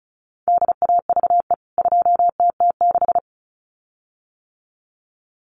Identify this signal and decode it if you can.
Morse code, heard as a keyed tone, about 34 wpm, 700 Hz. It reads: DAVE 2TT6
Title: CWT Exchange 35wpm